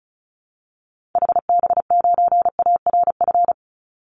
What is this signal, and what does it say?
Morse code, heard as a keyed tone, about 34 wpm, 700 Hz. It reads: HB9ARF
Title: Call Signs Repeated 3x - 35wpm